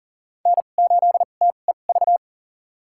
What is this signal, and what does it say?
Morse code, heard as a keyed tone, about 40 wpm, spaced as for 29 wpm, 700 Hz. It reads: N8TEV
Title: USA Call Signs - 5 Characters - 2x Character Spacing - 40wpm